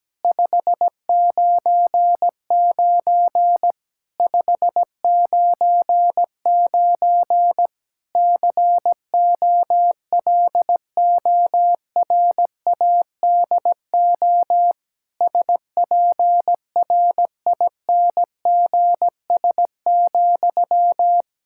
Morse code, heard as a keyed tone, about 17 wpm, 700 Hz. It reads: 599 599 COLORADO SPRINGS,
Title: K1USN SST QSO Practice 17wpm